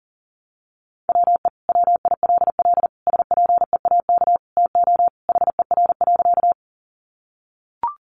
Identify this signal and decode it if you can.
Morse code, heard as a keyed tone, about 40 wpm, 700 Hz. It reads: WE WILL SPEAK TO HER.
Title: Sentences from Top 1100 Words 40wpm